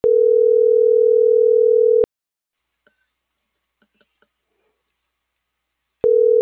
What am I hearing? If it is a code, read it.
Ringback tone (North America), 440 + 480 Hz.